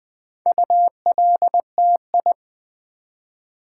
Morse code, heard as a keyed tone, about 20 wpm, 700 Hz. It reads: ULTI
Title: Build A Word - 4 Letters At A Time - 8-Letter Words - 20wpm